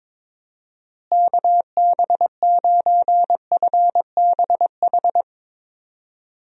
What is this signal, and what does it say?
Morse code, heard as a keyed tone, about 22 wpm, 700 Hz. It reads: KB9FBH
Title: USA Call Signs - 6 Characters - Repeated 2x - 22wpm